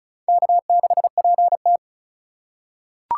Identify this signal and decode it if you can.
Morse code, heard as a keyed tone, about 34 wpm, 700 Hz. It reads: K6PT
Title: USA Call Signs - 4 Characters - 6x Character Spacing - 35wpm